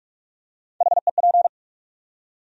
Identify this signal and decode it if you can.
Morse code, heard as a keyed tone, about 42 wpm, 700 Hz. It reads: HEP
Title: USA Call Sign Postfixes - 3 Letters 45wpm